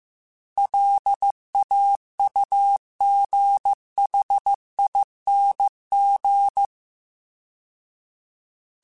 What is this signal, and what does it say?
Morse code, heard as a keyed tone, about 15 wpm, 800 Hz. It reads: LAUGHING